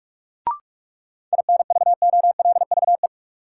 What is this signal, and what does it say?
Morse code, heard as a keyed tone, about 46 wpm, 700 Hz. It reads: INVOLVE
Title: Sentences from Top 1200 Words - New Word Drill Alt - 45wpm